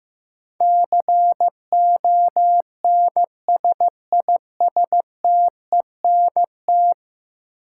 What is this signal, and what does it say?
Morse code, heard as a keyed tone, about 16 wpm, 700 Hz. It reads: CONSISTENT
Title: Sentences from Top 1500 Words - New Word Drill - 15wpm